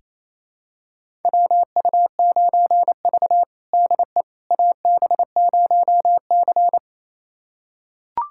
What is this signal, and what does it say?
Morse code, heard as a keyed tone, about 28 wpm, 700 Hz. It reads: WU9V DE AB0C
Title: USA Callsign DE Callsign - 4 Characters - 28wpm